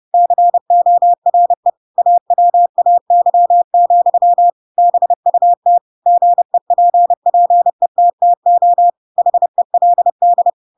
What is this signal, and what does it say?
Morse code, heard as a keyed tone, about 30 wpm, 700 Hz. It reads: CORE AWAY, BUT GEPPETTO HELD
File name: Pinocchio / Pinocchio ch7 30 30 ik1hge